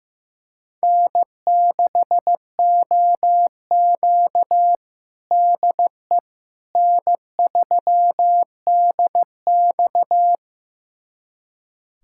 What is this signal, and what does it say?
Morse code, heard as a keyed tone, about 16 wpm, 700 Hz. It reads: N6OQ DE N3DX
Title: USA Callsign DE Callsign - 4 Characters - 15wpm